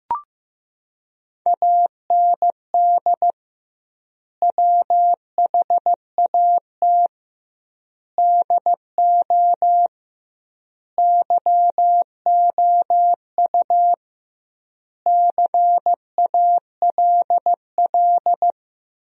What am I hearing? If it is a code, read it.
Morse code, heard as a keyed tone, about 15 wpm, 700 Hz. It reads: AND WHAT DO YOU CALL
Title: 2x Word-Spacing - Sentences from Top 100 Words - 15wpm